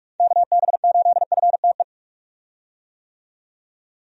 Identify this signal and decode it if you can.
Morse code, heard as a keyed tone, about 46 wpm, 700 Hz. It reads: KB8FTE
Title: USA Call Signs - 6 Characters - 45wpm